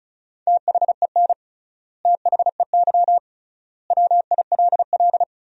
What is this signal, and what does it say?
Morse code, heard as a keyed tone, about 36 wpm, 700 Hz. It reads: THEN THEY WILL
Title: 3x Word-Spacing - Sentences from Top 200 Words - 35wpm